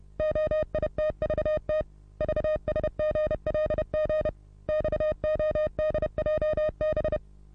Morse code, heard as a keyed tone, about 30 wpm, 620 Hz. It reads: OITVT VSGLG XODJB
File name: repos / Morsekurs / DJ4UF Morsen27